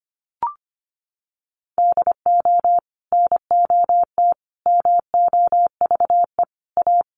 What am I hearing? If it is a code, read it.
Morse code, heard as a keyed tone, about 25 wpm, 700 Hz. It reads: DO NOT MOVE A
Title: Sentences from Top 1300 Words 25wpm